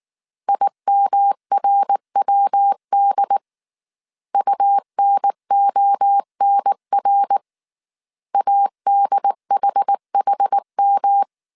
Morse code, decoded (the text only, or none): IMRWD UNONR ADHHM